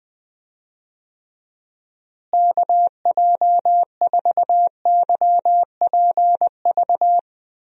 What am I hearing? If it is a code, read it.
Morse code, heard as a keyed tone, about 20 wpm, 700 Hz. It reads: KJ4YPV
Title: USA Call Signs - 6 Characters - Repeated 3x - 20wpm